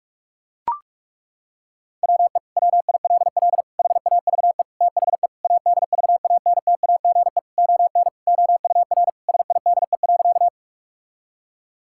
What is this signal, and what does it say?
Morse code, heard as a keyed tone, about 42 wpm, 700 Hz. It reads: WE WILL HAVE THE ADVANTAGE ON OUR SIDE.
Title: Sentences from Top 1000 Words - Review Part 6 - 45wpm